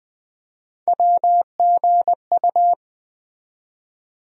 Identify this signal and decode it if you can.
Morse code, heard as a keyed tone, about 20 wpm, 700 Hz. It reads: WGU
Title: USA Call Sign Postfixes - 3 Letters 20wpm